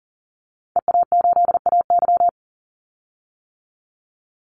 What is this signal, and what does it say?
Morse code, heard as a keyed tone, about 40 wpm, 700 Hz. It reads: EA8AY
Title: Call Signs - 40wpm